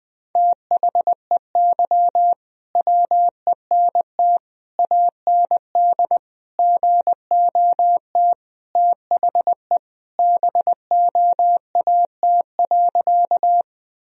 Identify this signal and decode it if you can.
Morse code, heard as a keyed tone, about 21 wpm, 700 Hz. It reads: THEY WENT AND GOT THE BOAT.
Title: Sentences from Top 1000 Words - Review Part 3 - 20wpm